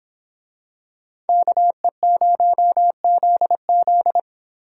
Morse code, heard as a keyed tone, about 26 wpm, 700 Hz. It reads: KE0ZZ
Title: Speed Racing - USA 2x2 Call Signs 22wpm